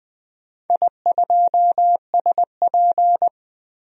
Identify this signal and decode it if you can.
Morse code, heard as a keyed tone, about 20 wpm, 700 Hz. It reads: I2SP
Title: DXCC-100 Call Signs - 4 Characters - 20wpm